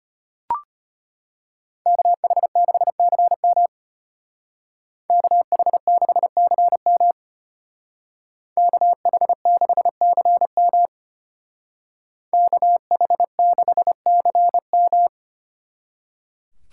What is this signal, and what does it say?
Morse code, heard as a keyed tone, about 30 wpm, 700 Hz. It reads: KH6CM KH6CM KH6CM KH6CM
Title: Speed Racing - USA 2x2 Call Signs 25wpm